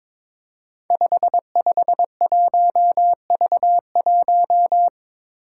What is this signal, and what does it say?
Morse code, heard as a keyed tone, about 22 wpm, 700 Hz. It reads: 551V1
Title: Five Challenging Letter-Number Combinations 22wpm